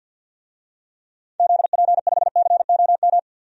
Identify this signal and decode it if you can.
Morse code, heard as a keyed tone, about 53 wpm, 700 Hz. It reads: ZP5COM
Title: DXCC-100 Call Signs - 6 Characters - 50wpm